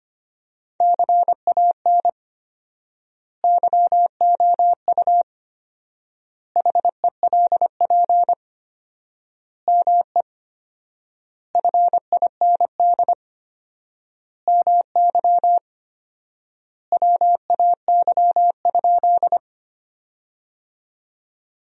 Morse code, heard as a keyed tone, about 25 wpm, 700 Hz. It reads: CAN YOU HELP ME FIND MY WAY?
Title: 4x Word-Spacing - Sentences from Top 200 Words - 25wpm